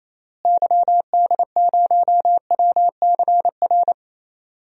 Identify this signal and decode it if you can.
Morse code, heard as a keyed tone, about 28 wpm, 700 Hz. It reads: YD0WCR
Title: DXCC-100 Call Signs - 6 Characters - Repeated 3x - 28wpm